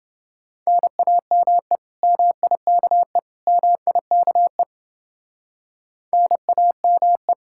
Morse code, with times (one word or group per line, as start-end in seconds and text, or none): 0.7-1.8 NAME
2.0-3.2 MIKE
3.5-4.6 MIKE
6.1-7.3 NAME